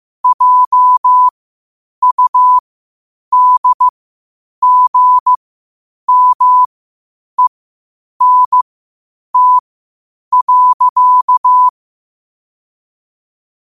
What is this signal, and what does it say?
Morse code, heard as a keyed tone, about 15 wpm, spaced as for 8 wpm, 1000 Hz. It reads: JUDGMENT.
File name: qotdfiles / 8wpm